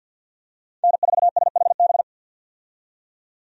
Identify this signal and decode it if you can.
Morse code, heard as a keyed tone, about 52 wpm, 700 Hz. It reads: N4SHB
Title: USA Call Signs - 5 Characters - 50wpm